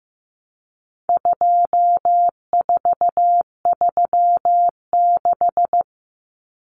Morse code, heard as a keyed tone, about 15 wpm, 700 Hz. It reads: 2436
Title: Numbers - 4-Digit 15wpm